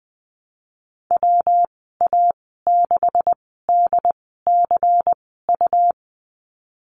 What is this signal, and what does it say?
Morse code, heard as a keyed tone, about 19 wpm, spaced as for 14 wpm, 700 Hz. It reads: WA6DCU
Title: 179 Lesson 29g - Slash - Call Signs 20wpm